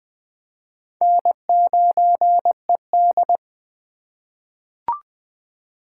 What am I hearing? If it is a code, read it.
Morse code, heard as a keyed tone, about 20 wpm, 700 Hz. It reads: N9ED
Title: USA Call Signs - 4 Characters - 2x Character Spacing - 20wpm